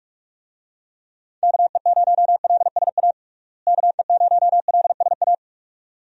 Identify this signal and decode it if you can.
Morse code, heard as a keyed tone, about 46 wpm, 700 Hz. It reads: KE0LSA KE0LSA
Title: USA Call Signs - 6 Characters - Repeated 2x - 45wpm